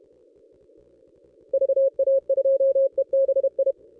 Morse code, heard as a keyed tone, about 32 wpm, 540 Hz. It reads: VA2EBI
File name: contests / ss-cw-k5tr-op-n5rz -1200